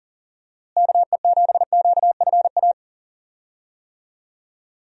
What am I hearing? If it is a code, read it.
Morse code, heard as a keyed tone, about 40 wpm, 700 Hz. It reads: KE7QFA
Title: USA Call Signs - 6 Characters - Repeated 2x - 40wpm